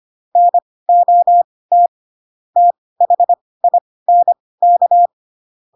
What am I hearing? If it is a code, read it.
Morse code, heard as a keyed tone, about 25 wpm, spaced as for 18 wpm, 700 Hz. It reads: NOT THINK
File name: Pinocchio / Pinocchio ch4 12 25 ik1hge